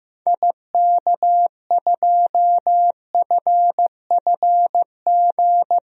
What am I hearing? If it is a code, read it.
Morse code, heard as a keyed tone, about 15 wpm, 700 Hz. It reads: IK2FFG
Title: DXCC-100 Call Signs - 6 Characters - Repeated 2x - 15wpm